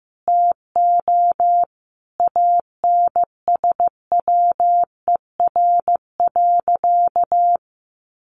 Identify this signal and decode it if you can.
Morse code, heard as a keyed tone, about 15 wpm, 700 Hz. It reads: TO ANSWER.
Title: Sentences from Top 1500 Words 15wpm